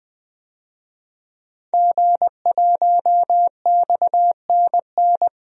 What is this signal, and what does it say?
Morse code, heard as a keyed tone, about 20 wpm, 700 Hz. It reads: G1XNN
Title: DXCC-100 Call Signs - 5 Characters - Repeated 2x - 20wpm